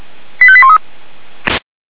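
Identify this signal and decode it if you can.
Five-tone selective call (ZVEI), heard as quick runs of five tones, tones of about 70 ms. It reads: 85813
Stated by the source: ZVEI five-tone selective call received by SDR.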